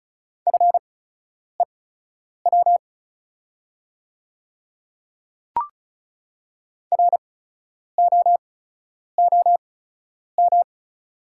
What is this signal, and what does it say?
Morse code, heard as a keyed tone, about 36 wpm, spaced as for 10 wpm, 690 Hz. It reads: FEW ROOM
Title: 8x Character Spacing - Top 500 Words - 35wpm